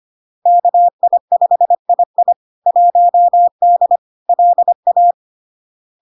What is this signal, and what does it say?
Morse code, heard as a keyed tone, about 25 wpm, 700 Hz. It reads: KI5II 1D LA
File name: Log-in-CW-25wpm-700-Hz